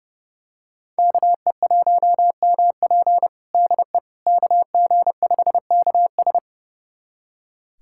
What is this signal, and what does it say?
Morse code, heard as a keyed tone, about 30 wpm, 700 Hz. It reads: KE1MP DE KG5KS
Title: USA Callsign DE Callsign - 5 Characters - 30wpm